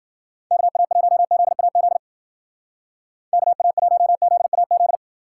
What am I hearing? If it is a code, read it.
Morse code, heard as a keyed tone, about 52 wpm, 700 Hz. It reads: KA1ZAZ KA1ZAZ